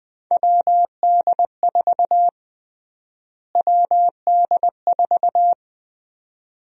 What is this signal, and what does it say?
Morse code, heard as a keyed tone, about 21 wpm, 700 Hz. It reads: WD4 WD4
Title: USA Call Sign Prefixes - Repeated 2x - 20wpm